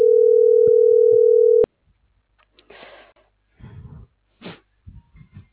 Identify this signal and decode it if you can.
Ringback tone (North America), 440 + 480 Hz.